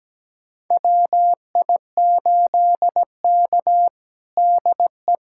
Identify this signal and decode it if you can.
Morse code, heard as a keyed tone, about 17 wpm, 700 Hz. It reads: WI8K DE
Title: USA Callsign DE Callsign 17wpm